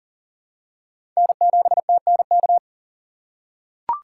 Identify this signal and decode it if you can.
Morse code, heard as a keyed tone, about 43 wpm, 700 Hz. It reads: N7TNK
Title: USA Call Signs - 5 Characters - 6x Character Spacing - 40wpm